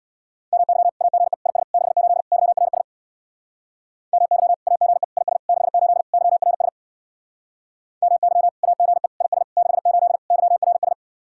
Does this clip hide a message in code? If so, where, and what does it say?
Yes: Morse code from 0.5 to 10.9 s, heard as a keyed tone, about 64 wpm, 700 Hz. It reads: MY AGE IS 68 YRS MY AGE IS 68 YRS MY AGE IS 68 YRS